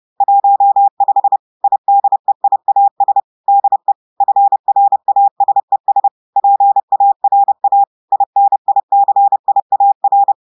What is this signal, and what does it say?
Morse code, heard as a keyed tone, about 30 wpm, 800 Hz. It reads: 15 IDEIAS DE FRASES PARA INICIAR
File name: portugais texte en clair 2 vitesse 30 mots minutes